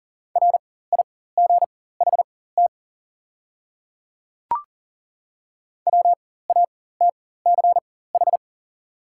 Morse code, heard as a keyed tone, about 40 wpm, spaced as for 19 wpm, 700 Hz. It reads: RIGHT WATCH
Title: 4x Character Spacing - Top 300 Words - 40wpm